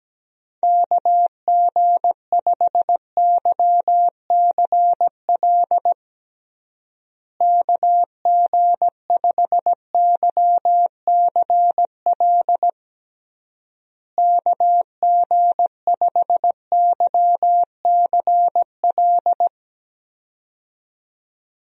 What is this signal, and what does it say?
Morse code, heard as a keyed tone, about 17 wpm, 700 Hz. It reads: KG5YCL KG5YCL KG5YCL
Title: USA Call Signs - 6 Characters - Repeated 3x - 17wpm